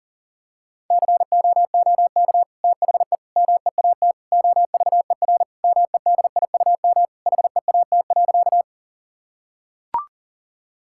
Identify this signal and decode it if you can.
Morse code, heard as a keyed tone, about 40 wpm, 700 Hz. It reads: COOK THE MEAT OVER MEDIUM HEAT.
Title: Sentences from Top 1000 Words - Review Part 6 - 40wpm